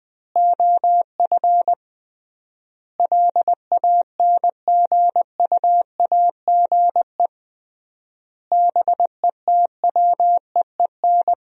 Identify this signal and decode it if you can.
Morse code, heard as a keyed tone, about 21 wpm, 700 Hz. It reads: OF LANGUAGE BETWEEN
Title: 3x Word-Spacing - Sentences from Top 500 Words - 20wpm